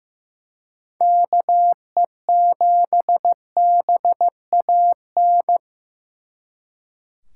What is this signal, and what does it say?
Morse code, heard as a keyed tone, about 18 wpm, 700 Hz. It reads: KE7BAN
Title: Speed Racing - USA 2x3 Call Signs 15wpm